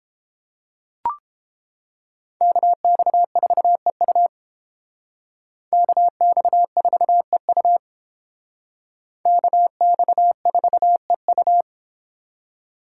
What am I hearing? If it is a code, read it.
Morse code, heard as a keyed tone, about 30 wpm, 700 Hz. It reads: KX4EU KX4EU KX4EU